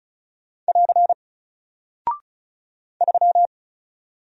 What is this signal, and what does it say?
Morse code, heard as a keyed tone, about 35 wpm, 700 Hz. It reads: <AR> 3
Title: Single Character (Letter:Number:Punctuation:Prosign) Alt 35wpm